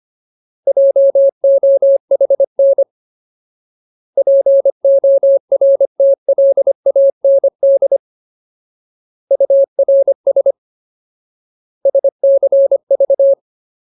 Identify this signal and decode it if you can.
Morse code, heard as a keyed tone, about 25 wpm, 550 Hz. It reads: JOHN PORTLAND URS SCV
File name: Ultimate Word List / 25WPM Ultimate